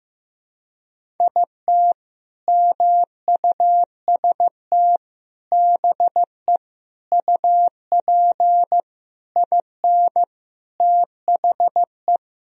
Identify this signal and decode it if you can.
Morse code, heard as a keyed tone, about 15 wpm, 700 Hz. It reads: IT MUST BE UP IN THE
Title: Sentences from Top 1000 Words - Review Part 1 - 15wpm